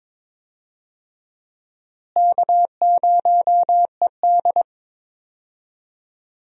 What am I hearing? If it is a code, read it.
Morse code, heard as a keyed tone, about 22 wpm, 700 Hz. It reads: K0ED
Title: USA Call Signs - 4 Characters - Repeated 3x - 22wpm